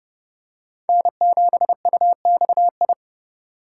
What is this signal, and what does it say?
Morse code, heard as a keyed tone, about 29 wpm, 700 Hz. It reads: N7UXI